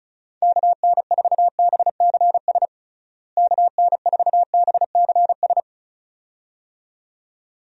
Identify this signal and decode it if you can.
Morse code, heard as a keyed tone, about 36 wpm, 700 Hz. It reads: KN4BCS KN4BCS
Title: USA Call Signs - 6 Characters - Repeated 2x - 35wpm